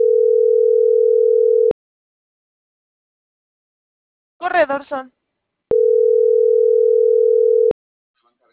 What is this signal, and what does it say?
Ringback tone (North America), 440 + 480 Hz.
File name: CENTRAL SAGOT / out adriana.poncec - .80452